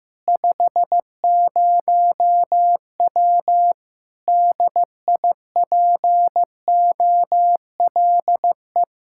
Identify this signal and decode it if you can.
Morse code, heard as a keyed tone, about 15 wpm, 700 Hz. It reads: 50W DIPOLE
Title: K1USN SST QSO Practice 15wpm